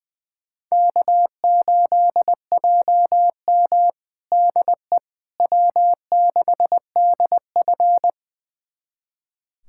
Morse code, heard as a keyed tone, about 21 wpm, 700 Hz. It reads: K8JM DE W6DF
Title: USA Callsign DE Callsign - 4 Characters - 20wpm